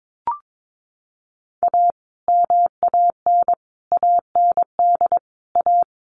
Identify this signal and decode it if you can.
Morse code, heard as a keyed tone, about 22 wpm, 700 Hz. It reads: A MAN AND A
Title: Sets of 5 Words Encore 22wpm